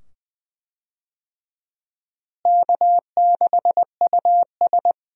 Morse code, heard as a keyed tone, about 20 wpm, 700 Hz. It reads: K6US
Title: USA Call Signs - 4 Characters - Repeated 2x - 20wpm